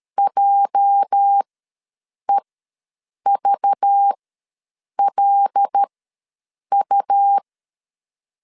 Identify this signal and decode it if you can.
Morse code, heard as a keyed tone, about 13 wpm, spaced as for 7 wpm, 800 Hz. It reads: JEVLU
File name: LEZIONI / 8 YJ45 / 8 YJ45 ws 12wpm 3146